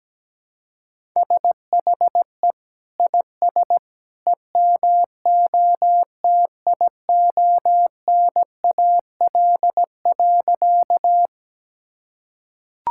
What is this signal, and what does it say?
Morse code, heard as a keyed tone, about 17 wpm, 700 Hz. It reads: SHE IS EMOTIONAL.
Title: Sentences from Top 1000 Words - Review Part 5 - 17wpm